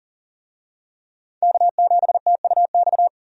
Morse code, heard as a keyed tone, about 40 wpm, 700 Hz. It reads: K7TUX
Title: USA Call Signs - 5 Characters - 40wpm